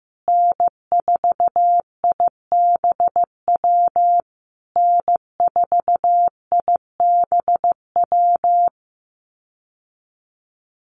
Morse code, heard as a keyed tone, about 15 wpm, 700 Hz. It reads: N4IBW N4IBW
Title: USA Call Signs Repeated 3x 15wpm